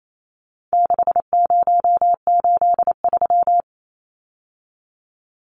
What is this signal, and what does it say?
Morse code, heard as a keyed tone, about 28 wpm, 700 Hz. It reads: 6083
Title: Numbers - 4-Digit 28wpm